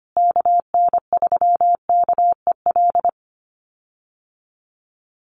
Morse code, heard as a keyed tone, about 25 wpm, 700 Hz. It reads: KN3KEL
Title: USA Call Signs Repeated 3x 25wpm